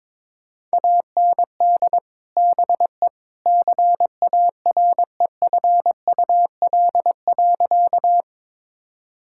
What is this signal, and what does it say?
Morse code, heard as a keyed tone, about 22 wpm, 700 Hz. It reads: AND BE CAREFUL.
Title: Sentences from Top 1600 Words 22wpm